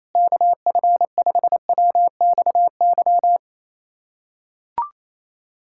Morse code, heard as a keyed tone, about 28 wpm, 700 Hz. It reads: KF5WXY
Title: USA Callsign DE Callsign 28wpm